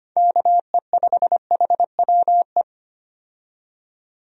Morse code, heard as a keyed tone, about 25 wpm, 700 Hz. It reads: KE5HWE
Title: USA Call Signs - 6 Characters - 25wpm